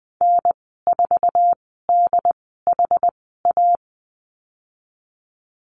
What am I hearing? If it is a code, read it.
Morse code, heard as a keyed tone, about 20 wpm, spaced as for 14 wpm, 700 Hz. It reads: N4DHA
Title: 088 Lesson 15g - Number 5 - Call Signs 20wpm